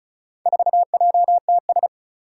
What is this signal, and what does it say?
Morse code, heard as a keyed tone, about 36 wpm, 700 Hz. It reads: 4JTS
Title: Four Challenging Letter-Number Combinations 35wpm